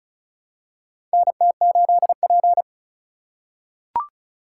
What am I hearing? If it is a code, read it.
Morse code, heard as a keyed tone, about 34 wpm, 700 Hz. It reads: NT8P
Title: USA Call Signs - 4 Characters - 2x Character Spacing - 35wpm